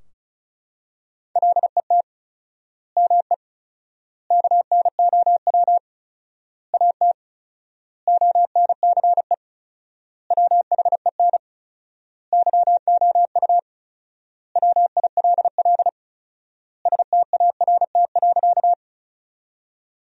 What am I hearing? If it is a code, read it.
Morse code, heard as a keyed tone, about 34 wpm, 700 Hz. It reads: LET ME KNOW AT ONCE WHEN YOU WILL START.
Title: 4x Word-Spacing - Sentences from Top 300 Words - 35wpm